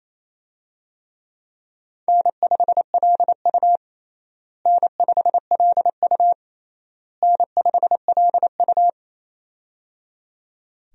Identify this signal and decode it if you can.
Morse code, heard as a keyed tone, about 28 wpm, 700 Hz. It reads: N5LU N5LU N5LU
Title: USA Call Signs - 4 Characters - Repeated 3x - 28wpm